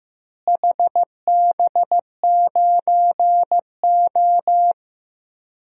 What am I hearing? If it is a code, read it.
Morse code, heard as a keyed tone, about 15 wpm, 700 Hz. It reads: HB9O
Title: DXCC-100 Call Signs - 4 Characters - Repeated 3x - 15wpm